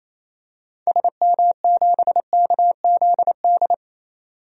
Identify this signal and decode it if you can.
Morse code, heard as a keyed tone, about 28 wpm, 700 Hz. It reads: SM7KZD
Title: DXCC-100 Call Signs - 6 Characters - 28wpm